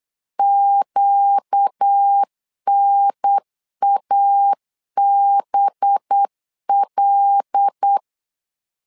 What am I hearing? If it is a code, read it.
Morse code, heard as a keyed tone, about 8 wpm, 800 Hz. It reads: QNABL
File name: LEZIONI / 5 LBVC / 5 LB 8wpm 2816